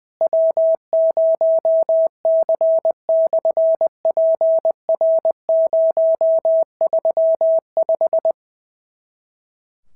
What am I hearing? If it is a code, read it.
Morse code, heard as a keyed tone, about 20 wpm, 650 Hz. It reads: W0C/PR035
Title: SOTA QSO Practice 20wpm